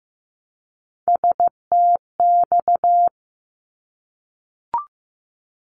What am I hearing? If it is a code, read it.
Morse code, heard as a keyed tone, about 15 wpm, 700 Hz. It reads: STX
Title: ARRL Sections 15wpm